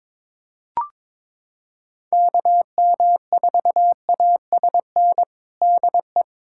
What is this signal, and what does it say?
Morse code, heard as a keyed tone, about 22 wpm, 700 Hz. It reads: KM4ASN DE
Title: USA Callsign DE Callsign - 6 Characters - 22wpm